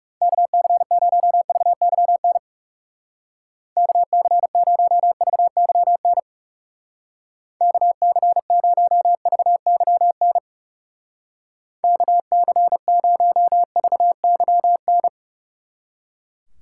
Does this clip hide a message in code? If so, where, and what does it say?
Yes: Morse code from 0.2 to 15.1 s, heard as a keyed tone, about 38 wpm, 700 Hz. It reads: KC0VYN KC0VYN KC0VYN KC0VYN